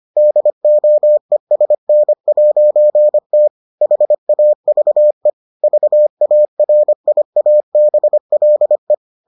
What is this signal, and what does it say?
Morse code, heard as a keyed tone, about 25 wpm, 600 Hz. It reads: DOESN'T HAVE VARIABLE